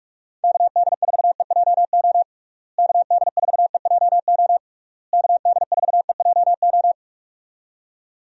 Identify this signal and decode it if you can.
Morse code, heard as a keyed tone, about 47 wpm, 700 Hz. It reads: KD4EJO KD4EJO KD4EJO
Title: USA Call Signs - 6 Characters - Repeated 3x - 45wpm